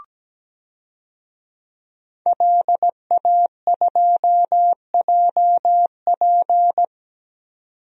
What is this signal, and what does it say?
Morse code, heard as a keyed tone, about 17 wpm, 700 Hz. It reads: LA2JP
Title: DXCC-100 Call Signs - 5 Characters - 17wpm